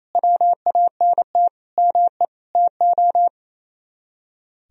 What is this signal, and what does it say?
Morse code, heard as a keyed tone, about 34 wpm, 700 Hz. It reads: WANT ME TO
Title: Speed-Racing - Sets of 4 Words 28wpm